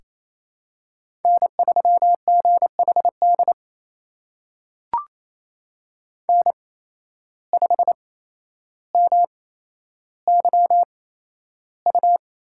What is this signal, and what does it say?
Morse code, heard as a keyed tone, about 28 wpm, spaced as for 12 wpm, 700 Hz. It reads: N3GHD N5MYU
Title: USA Call Signs - 5 Characters - 8x Character Spacing - 28wpm